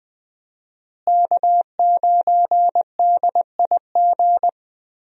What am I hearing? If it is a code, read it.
Morse code, heard as a keyed tone, about 20 wpm, 700 Hz. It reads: K9DIG
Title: USA Call Signs - 5 Characters - Repeated 3x - 20wpm